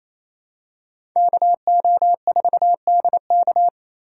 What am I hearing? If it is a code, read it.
Morse code, heard as a keyed tone, about 28 wpm, 700 Hz. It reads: KO4DK
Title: USA Call Signs - 5 Characters - 2x Character Spacing - 28wpm